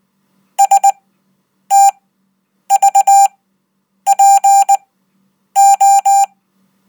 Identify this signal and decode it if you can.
Morse code, heard as a keyed tone, about 19 wpm, spaced as for 8 wpm, 790 Hz. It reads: STVPO
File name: Code